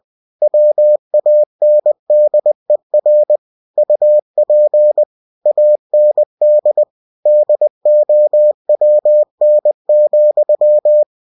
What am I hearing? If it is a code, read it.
Morse code, heard as a keyed tone, about 20 wpm, 600 Hz. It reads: WANDER UP AND DOWN,
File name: cw-books / winnie-the-pooh / winnie-20wpm-page-0039